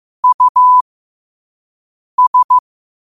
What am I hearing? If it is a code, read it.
Morse code, heard as a keyed tone, about 15 wpm, spaced as for 5 wpm, 1000 Hz. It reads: US
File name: qotdfiles / 5wpm